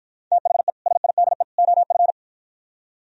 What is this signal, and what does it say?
Morse code, heard as a keyed tone, about 52 wpm, 700 Hz. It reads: THE SIDE OF